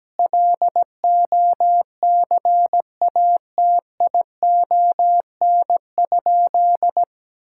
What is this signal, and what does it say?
Morse code, heard as a keyed tone, about 17 wpm, 700 Hz. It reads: LOCATION?
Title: Sentences from Top 1000 Words - Review Part 6 - 17wpm